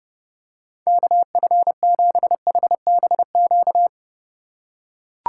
Morse code, heard as a keyed tone, about 32 wpm, 700 Hz. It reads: KF7HBQ